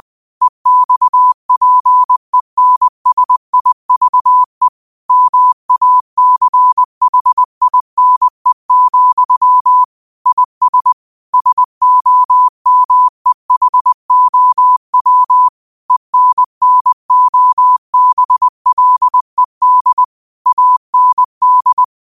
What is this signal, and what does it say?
Morse code, heard as a keyed tone, about 20 wpm, 1000 Hz. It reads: EXPENSIVE MACHINE, IS SOMEHOW ENNOBLED AND
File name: qotdfiles / 20wpm